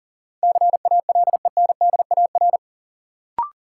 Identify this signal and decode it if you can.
Morse code, heard as a keyed tone, about 39 wpm, 700 Hz. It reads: CALENDAR
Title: Sentences from Top 1600 Words - New Word Drill - 40wpm